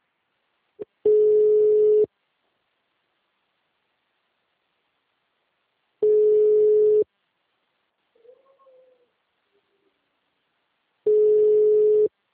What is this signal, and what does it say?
Ringback tone (Europe, 425 Hz).